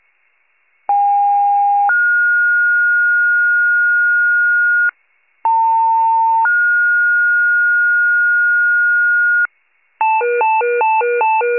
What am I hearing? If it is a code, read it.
Two-tone page, heard as a one-second tone, then a three-second tone, about 810 then 1430 Hz, about 900 then 1430 Hz. It is heard from 0.9 to 9.5 s.